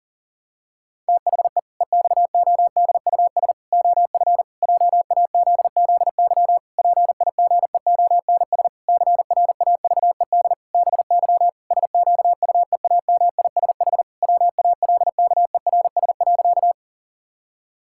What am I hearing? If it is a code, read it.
Morse code, heard as a keyed tone, about 40 wpm, 700 Hz. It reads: THE EXODUS OF JAZZY PIGEONS CRAVED BY SQUEAMISH WALKERS.
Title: Pangram Challenge v2 - 40wpm